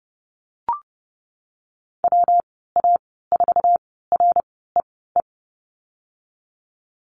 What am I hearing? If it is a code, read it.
Morse code, heard as a keyed tone, about 32 wpm, spaced as for 17 wpm, 700 Hz. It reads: WA4REE
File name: Lesson 19g - Letter w - Call Signs 30wpm